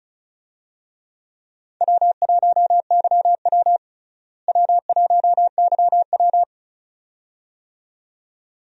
Morse code, heard as a keyed tone, about 34 wpm, 700 Hz. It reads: W1YW W1YW
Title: USA Call Signs - 4 Characters - Repeated 2x - 35wpm